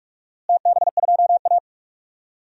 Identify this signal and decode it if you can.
Morse code, heard as a keyed tone, about 44 wpm, 700 Hz. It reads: TB2A
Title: DXCC-100 Call Signs - 4 Characters - Repeated 3x - 45wpm